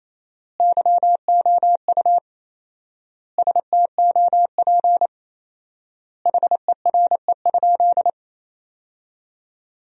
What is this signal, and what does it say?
Morse code, heard as a keyed tone, about 28 wpm, 700 Hz. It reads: YOU STOP HERE?
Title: 4x Word-Spacing - Sentences from Top 300 Words - 28wpm